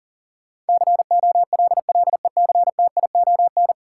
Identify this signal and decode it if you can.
Morse code, heard as a keyed tone, about 40 wpm, 700 Hz. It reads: COLLECTION
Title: Sentences from Top 1200 Words - New Word Drill Alt - 40wpm